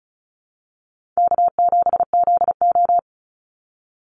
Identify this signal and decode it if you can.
Morse code, heard as a keyed tone, about 34 wpm, 700 Hz. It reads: K7ZO
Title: NA SOTA Chaser Callsigns 35wpm